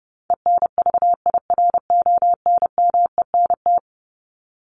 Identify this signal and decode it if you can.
Morse code, heard as a keyed tone, about 30 wpm, 700 Hz. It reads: ENVIRONMENT
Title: Top 501-750 Words 30wpm